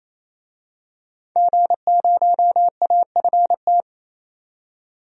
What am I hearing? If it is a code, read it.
Morse code, heard as a keyed tone, about 28 wpm, 700 Hz. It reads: G0AFT
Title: DXCC-100 Call Signs - 5 Characters - 28wpm